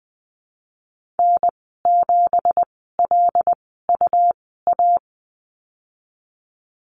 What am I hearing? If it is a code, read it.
Morse code, heard as a keyed tone, about 19 wpm, spaced as for 14 wpm, 700 Hz. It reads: N7LUA
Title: 193 Lesson 31g - Letter v - Call Signs 20wpm